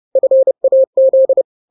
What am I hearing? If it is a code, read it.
Morse code, heard as a keyed tone, about 30 wpm, 530 Hz. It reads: FAZ
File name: daily / 29wpm